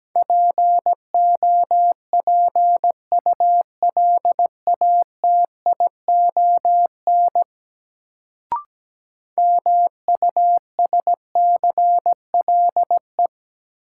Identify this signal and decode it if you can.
Morse code, heard as a keyed tone, about 17 wpm, 700 Hz. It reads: POPULATION MUSCLE
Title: Sentences from Top 1300 Words - New Word Drill Alt - 17wpm